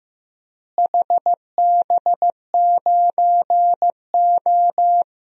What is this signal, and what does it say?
Morse code, heard as a keyed tone, about 15 wpm, 700 Hz. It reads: HB9O
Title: DXCC-100 Call Signs - 4 Characters - Repeated 2x - 15wpm